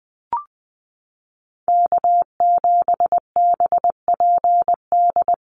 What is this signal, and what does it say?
Morse code, heard as a keyed tone, about 19 wpm, 700 Hz. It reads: K7BPD
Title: USA Call Signs 20wpm